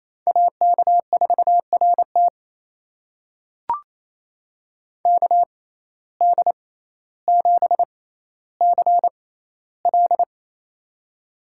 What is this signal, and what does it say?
Morse code, heard as a keyed tone, about 28 wpm, spaced as for 14 wpm, 700 Hz. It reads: AK4RT KD7CL
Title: USA Call Signs - 5 Characters - 6x Character Spacing - 28wpm